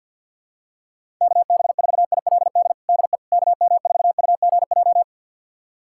Morse code, heard as a keyed tone, about 51 wpm, 700 Hz. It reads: KB4ILD DE KM4UGJ
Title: USA Callsign DE Callsign - 6 Characters - 50wpm